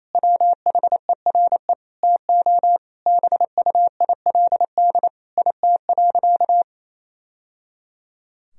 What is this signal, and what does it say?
Morse code, heard as a keyed tone, about 28 wpm, 700 Hz. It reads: WHERE TO BUILD IT.
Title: Sentences from Top 1000 Words - Review Part 1 - 28wpm